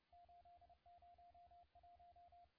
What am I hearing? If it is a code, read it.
Morse code, heard as a keyed tone, about 31 wpm, 700 Hz, with radio noise. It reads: 801
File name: la4c / sac-cw-2017 / CONTEST AUDIO 2100Z